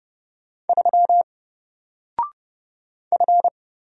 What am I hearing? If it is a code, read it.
Morse code, heard as a keyed tone, about 31 wpm, 700 Hz. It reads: 3 F
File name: Single-Letter-Number-30wpm